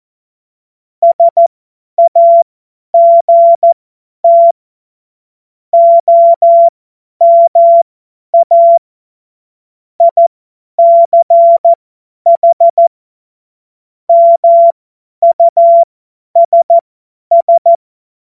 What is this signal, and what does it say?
Morse code, heard as a keyed tone, about 14 wpm, spaced as for 10 wpm, 680 Hz. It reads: SAGT OMA ICH MUSS
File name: Morsekurs / Witze / Witz 12. Oktober 2021, 50 F70